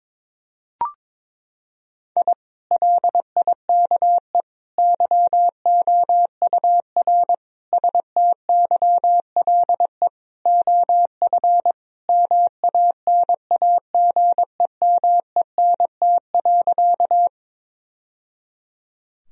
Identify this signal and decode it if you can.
Morse code, heard as a keyed tone, about 22 wpm, 700 Hz. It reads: I LIKE YOUR STYLE OF MANAGEMENT.
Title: Sentences from Top 1000 Words - Review Part 4 - 22wpm